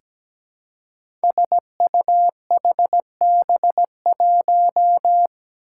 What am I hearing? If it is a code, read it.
Morse code, heard as a keyed tone, about 17 wpm, 700 Hz. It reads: SUHB1
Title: Five Challenging Letter-Number Combinations 17wpm